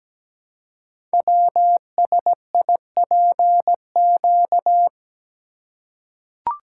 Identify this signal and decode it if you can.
Morse code, heard as a keyed tone, about 17 wpm, 700 Hz. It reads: WSIPQ
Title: Code Groups - 5 Letters - 17wpm